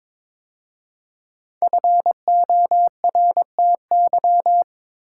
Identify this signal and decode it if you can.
Morse code, heard as a keyed tone, about 22 wpm, 700 Hz. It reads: FORTY
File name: 5 Letter Words - First 3 Letters Given - 22wpm